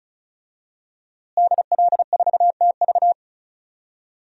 Morse code, heard as a keyed tone, about 36 wpm, 700 Hz. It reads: DL4TV
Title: DXCC-100 Call Signs - 5 Characters - Repeated 3x - 35wpm